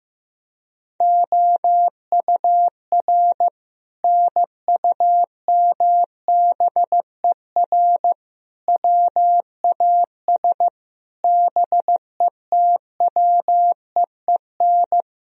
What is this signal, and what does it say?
Morse code, heard as a keyed tone, about 15 wpm, 700 Hz. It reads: OUR NUMBER WAS BETWEEN
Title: Sentences from Top 1000 Words - Review Part 2 - 15wpm